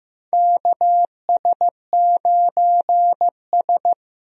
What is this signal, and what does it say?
Morse code, heard as a keyed tone, about 15 wpm, 700 Hz. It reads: KS9S
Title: USA Call Signs - 4 Characters - 15wpm